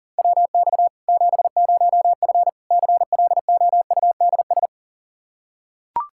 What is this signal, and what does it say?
Morse code, heard as a keyed tone, about 37 wpm, 700 Hz. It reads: WX 70F CLOUDS
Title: K1USN SST QSO Practice 40wpm